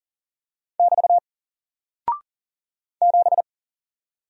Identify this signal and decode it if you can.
Morse code, heard as a keyed tone, about 40 wpm, 700 Hz. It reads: <BT> 7
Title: Single Character (Letter:Number:Punctuation:Prosign) Alt 40wpm